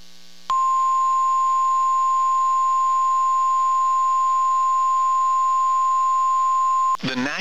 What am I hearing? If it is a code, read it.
Emergency Alert System (SAME) tones (the 1050 Hz attention tone). It is heard from 0.4 to 7.0 s.
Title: KEC94 NOAA Weather Radio